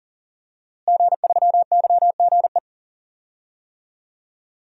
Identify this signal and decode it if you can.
Morse code, heard as a keyed tone, about 39 wpm, 690 Hz. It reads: G3YGE
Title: DXCC-100 Call Signs - 5 Characters - Repeated 2x - 40wpm